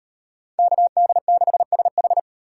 Morse code, heard as a keyed tone, about 36 wpm, 700 Hz. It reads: KD6SH